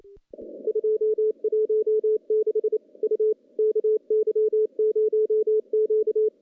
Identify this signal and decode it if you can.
Morse code, heard as a keyed tone, about 28 wpm, 430 Hz. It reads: 216 U KY0Q
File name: contests / CW-SS-k5tr- -0400